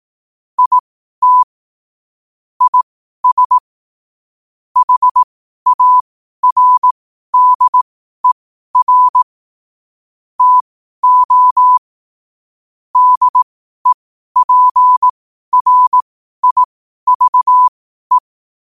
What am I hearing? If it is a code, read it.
Morse code, heard as a keyed tone, about 18 wpm, spaced as for 12 wpm, 1000 Hz. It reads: IT IS HARDER TO DEPRIVE
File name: qotdfiles / 12wpm